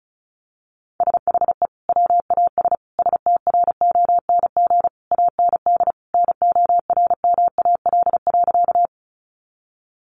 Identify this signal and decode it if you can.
Morse code, heard as a keyed tone, about 34 wpm, 700 Hz. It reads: SHE WAS STRONG AND NORMAL.
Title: Sentences from Top 1200 Words 35wpm